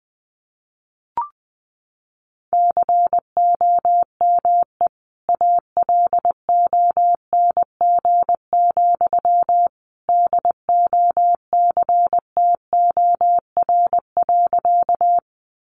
Morse code, heard as a keyed tone, about 20 wpm, 700 Hz. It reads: COME ALONG, DOCTOR.
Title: Sentences from Top 1500 Words 20wpm